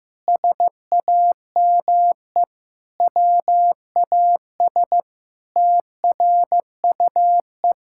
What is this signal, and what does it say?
Morse code, heard as a keyed tone, about 16 wpm, 700 Hz. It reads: SAME WAS TRUE
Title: Sentences from Top 1000 Words - Review Part 6 - 15wpm